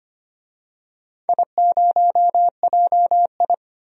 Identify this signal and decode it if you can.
Morse code, heard as a keyed tone, about 25 wpm, 700 Hz. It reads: I0JI
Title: DXCC-100 Call Signs - 4 Characters - 25wpm